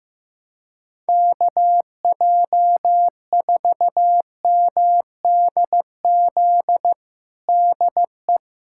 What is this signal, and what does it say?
Morse code, heard as a keyed tone, about 15 wpm, 700 Hz. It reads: KJ4MDZ DE
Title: USA Callsign DE Callsign - 6 Characters - 15wpm